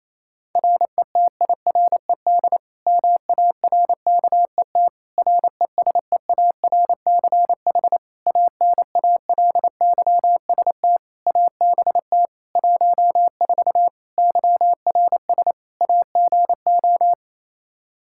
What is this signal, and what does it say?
Morse code, heard as a keyed tone, about 28 wpm, 700 Hz. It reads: RETIRED MARKET RESEARCH ANALYST ABT 14 YRS AGO
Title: Practice QSO - Exchange 3 - More Info - 28wpm